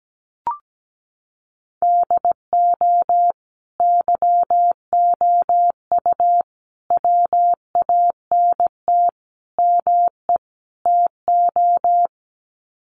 Morse code, heard as a keyed tone, about 17 wpm, 700 Hz. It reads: DO YOU WANT ME TO
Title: Sets of 5 Words 17wpm